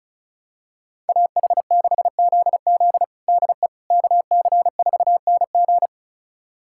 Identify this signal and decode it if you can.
Morse code, heard as a keyed tone, about 36 wpm, 700 Hz. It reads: AH6ZZ DE KC4NG
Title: USA Callsign DE Callsign - 5 Characters - 35wpm